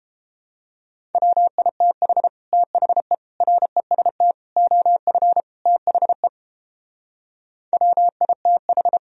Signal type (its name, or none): Morse code